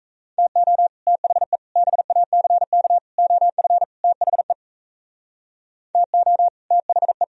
Morse code, heard as a keyed tone, about 36 wpm, 700 Hz. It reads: TO THE BACK OF THE TO THE